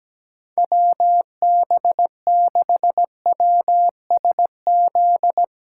Morse code, heard as a keyed tone, about 17 wpm, 700 Hz. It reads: WB6WSZ
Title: USA Callsign DE Callsign 17wpm